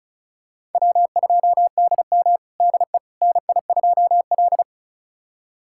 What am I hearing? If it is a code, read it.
Morse code, heard as a keyed tone, about 36 wpm, 700 Hz. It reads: W2DM DE NI2L
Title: USA Callsign DE Callsign - 4 Characters - 35wpm